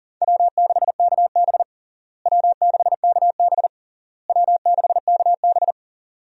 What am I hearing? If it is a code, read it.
Morse code, heard as a keyed tone, about 40 wpm, 700 Hz. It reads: W6KB W6KB W6KB
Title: USA Call Signs - 4 Characters - Repeated 3x - 40wpm